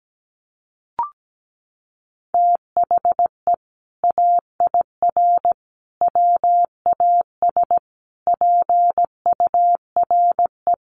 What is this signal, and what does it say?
Morse code, heard as a keyed tone, about 17 wpm, 700 Hz. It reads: THE AIR WAS PURE
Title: Sentences from Top 1500 Words 17wpm